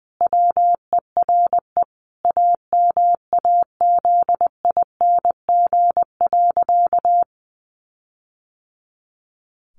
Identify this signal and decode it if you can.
Morse code, heard as a keyed tone, about 20 wpm, 700 Hz. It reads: WERE AMAZING.
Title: Sentences from Top 1500 Words 20wpm